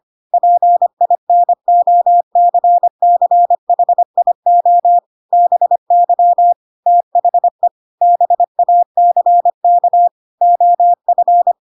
Morse code, heard as a keyed tone, about 25 wpm, 700 Hz. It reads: PINOCCHIO BY THE BACK OF
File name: Pinocchio / Pinocchio ch3 25 25 ik1hge